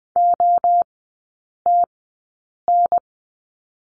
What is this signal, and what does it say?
Morse code, heard as a keyed tone, about 20 wpm, spaced as for 8 wpm, 700 Hz. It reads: OTN
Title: 047 Lesson 9e - Number 4 - Three Characters 20wpm